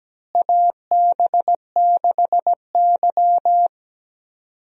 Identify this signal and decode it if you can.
Morse code, heard as a keyed tone, about 17 wpm, 700 Hz. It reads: AB6Y
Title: USA Call Signs - 4 Characters - Repeated 2x - 17wpm